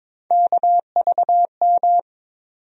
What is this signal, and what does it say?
Morse code, heard as a keyed tone, about 22 wpm, 700 Hz. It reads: KVM
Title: USA Call Sign Postfixes - 3 Letters 22wpm